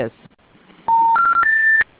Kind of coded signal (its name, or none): telephone call progress tone